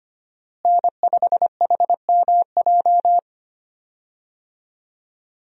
Morse code, heard as a keyed tone, about 25 wpm, 700 Hz. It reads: N5HMJ
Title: USA Call Signs - 5 Characters - Repeated 3x - 25wpm